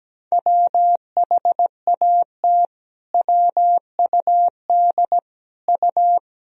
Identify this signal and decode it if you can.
Morse code, heard as a keyed tone, about 17 wpm, 700 Hz. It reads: WHAT WUD U
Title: Sentences from Top 300 Words and Abbreviations 17wpm